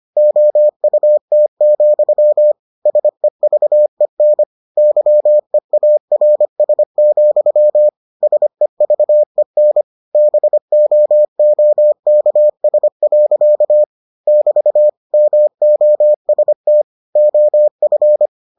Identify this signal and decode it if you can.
Morse code, heard as a keyed tone, about 25 wpm, 600 Hz. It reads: OUT, SEVEN YEARS, SEVEN BOOKS. <BT> MOST OF